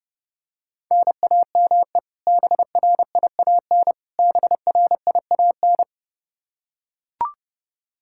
Morse code, heard as a keyed tone, about 32 wpm, 700 Hz. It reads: NAME BRIAN BRIAN
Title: K1USN SST QSO Practice 30wpm